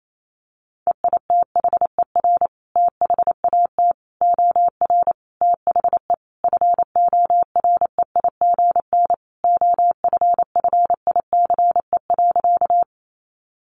Morse code, heard as a keyed tone, about 28 wpm, 700 Hz. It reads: EITHER THAT OR THE FOREIGN OFFICE.
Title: Sentences from Top 1400 Words 28wpm